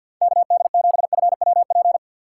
Morse code, heard as a keyed tone, about 64 wpm, 700 Hz. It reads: KD7FWP